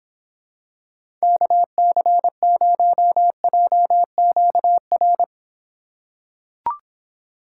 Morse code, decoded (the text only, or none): KC0JQR